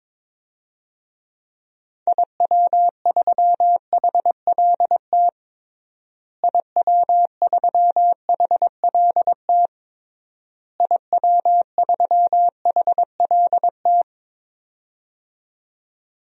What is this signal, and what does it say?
Morse code, heard as a keyed tone, about 22 wpm, 700 Hz. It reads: IW3HLT IW3HLT IW3HLT
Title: DXCC-100 Call Signs - 6 Characters - Repeated 3x - 22wpm